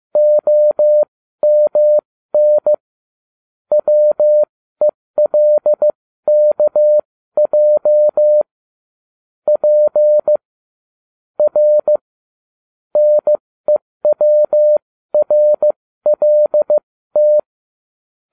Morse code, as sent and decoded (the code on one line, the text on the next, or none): --- -- -. / .-- . .-.. -.- .--- / .--. / .-. / -. .-- .-. .-.. -
OMN WELKJ P R NEWRLT